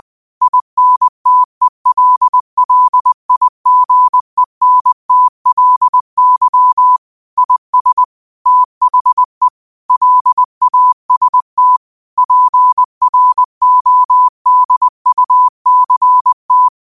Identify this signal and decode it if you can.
Morse code, heard as a keyed tone, about 20 wpm, 1000 Hz. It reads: INTELLIGENTLY IS THE LAST PRODUCT
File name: qotdfiles / 20wpm